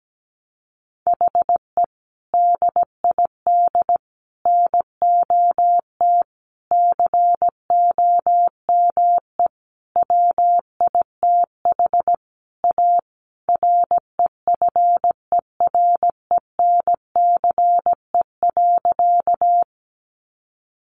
Morse code, heard as a keyed tone, about 17 wpm, 700 Hz. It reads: HE DID NOT COME WITH A REFERENCE.
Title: Sentences from Top 1200 Words 17wpm